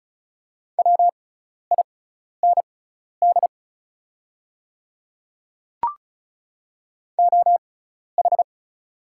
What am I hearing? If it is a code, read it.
Morse code, heard as a keyed tone, about 36 wpm, spaced as for 12 wpm, 700 Hz. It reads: WIND OH
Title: 6x Character Spacing - Top 500 Words - 35wpm